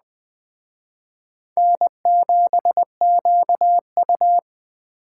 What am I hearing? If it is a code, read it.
Morse code, heard as a keyed tone, about 20 wpm, 700 Hz. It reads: N7QU
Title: Speed Racing - USA 1x2 Call Signs 17wpm